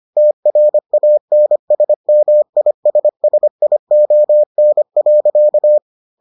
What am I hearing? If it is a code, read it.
Morse code, heard as a keyed tone, about 25 wpm, 600 Hz. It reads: TRANSMISSION.